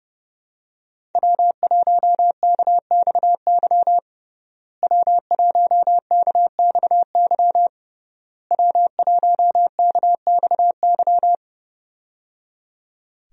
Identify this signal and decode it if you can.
Morse code, heard as a keyed tone, about 31 wpm, 700 Hz. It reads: W1KXY W1KXY W1KXY
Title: USA Call Signs - 5 Characters - Repeated 3x - 30wpm